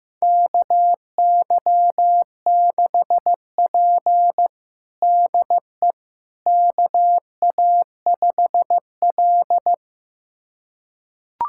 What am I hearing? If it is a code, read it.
Morse code, heard as a keyed tone, about 15 wpm, 700 Hz. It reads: KY6P DE KA5L
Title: USA Callsign DE Callsign - 4 Characters - 15wpm